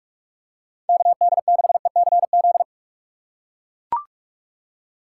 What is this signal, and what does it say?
Morse code, heard as a keyed tone, about 41 wpm, 700 Hz. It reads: KD6ECZ
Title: USA Call Signs - 6 Characters - 4x Character Spacing - 45wpm